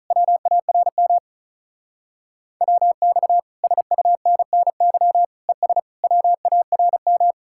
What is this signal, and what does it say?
Morse code, heard as a keyed tone, about 45 wpm, 700 Hz. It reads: WARM WX SUNNY ES WARM